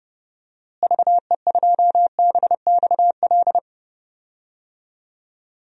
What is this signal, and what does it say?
Morse code, heard as a keyed tone, about 32 wpm, 700 Hz. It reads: VE2BXL
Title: DXCC-100 Call Signs - 6 Characters - Repeated 2x - 30wpm